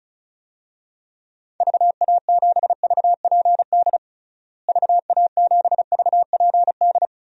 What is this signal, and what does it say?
Morse code, heard as a keyed tone, about 36 wpm, 700 Hz. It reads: VA7VPD VA7VPD
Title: DXCC-100 Call Signs - 6 Characters - Repeated 3x - 35wpm